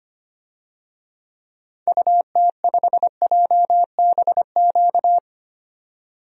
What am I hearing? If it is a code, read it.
Morse code, heard as a keyed tone, about 25 wpm, 700 Hz. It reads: UT5JBQ
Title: DXCC-100 Call Signs - 6 Characters - 25wpm